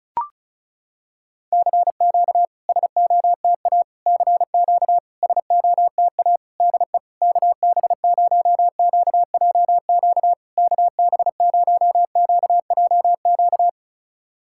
Morse code, heard as a keyed tone, about 36 wpm, 700 Hz. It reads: CQ SOTA CQ SOTA DE KB0QJQ KB0QJQ
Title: SOTA QSO Practice 35wpm